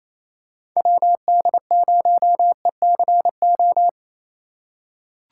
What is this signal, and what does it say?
Morse code, heard as a keyed tone, about 28 wpm, 700 Hz. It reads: WD0ECO
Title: K1USN SST QSO Practice - 28wpm